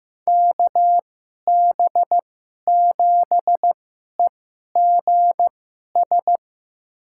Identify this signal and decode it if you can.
Morse code, heard as a keyed tone, about 15 wpm, spaced as for 11 wpm, 700 Hz. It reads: KB7EGS
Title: USA Call Signs - 6 Characters - 2x Character Spacing - 15wpm